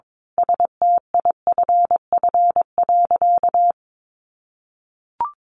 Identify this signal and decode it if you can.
Morse code, heard as a keyed tone, about 22 wpm, 700 Hz. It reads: STIFF.
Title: A Sound Of Thunder 22wpm